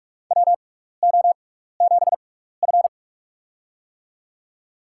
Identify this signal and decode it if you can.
Morse code, heard as a keyed tone, about 44 wpm, spaced as for 15 wpm, 700 Hz. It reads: WO7F
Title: USA Call Signs - 4 Characters - 6x Character Spacing - 45wpm